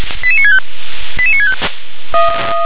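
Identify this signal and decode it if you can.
Five-tone selective call (ZVEI), heard as quick runs of five tones, tones of about 70 ms. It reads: 88965 88965
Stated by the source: ZVEI five-tone call 88961 received on an SDR.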